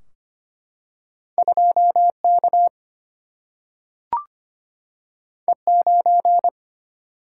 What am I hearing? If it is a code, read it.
Morse code, heard as a keyed tone, about 25 wpm, 700 Hz. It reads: 2K E9
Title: Single-Letter Single-Number 25wpm